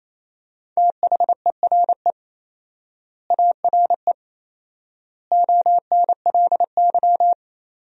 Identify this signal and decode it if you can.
Morse code, heard as a keyed tone, about 28 wpm, 700 Hz. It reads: THERE ARE ONLY
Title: 4x Word-Spacing - Sentences from Top 300 Words - 28wpm